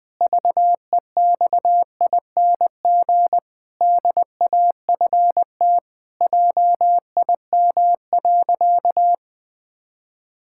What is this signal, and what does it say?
Morse code, heard as a keyed tone, about 21 wpm, 700 Hz. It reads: VEXING DAFT JIM.
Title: Pangram Challenge v2 - 20wpm